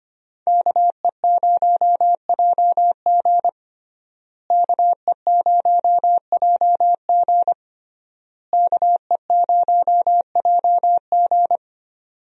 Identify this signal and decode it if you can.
Morse code, heard as a keyed tone, about 25 wpm, 700 Hz. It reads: KE0JG KE0JG KE0JG
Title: USA Call Signs - 5 Characters - Repeated 3x - 25wpm